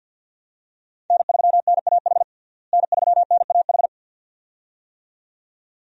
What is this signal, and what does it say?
Morse code, heard as a keyed tone, about 53 wpm, 700 Hz. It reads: N3NAH N3NAH
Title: USA Call Signs - 5 Characters - Repeated 2x - 50wpm